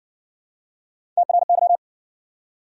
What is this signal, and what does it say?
Morse code, heard as a keyed tone, about 60 wpm, 700 Hz. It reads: TRY